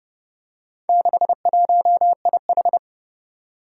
Morse code, heard as a keyed tone, about 30 wpm, 700 Hz. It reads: 61IH
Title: Four Challenging Letter-Number Combinations 30wpm